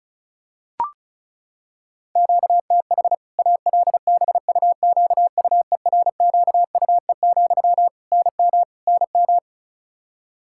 Morse code, heard as a keyed tone, about 36 wpm, 700 Hz. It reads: QTH ALBUQUERQUE, NM NM
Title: K1USN SST QSO Practice 35wpm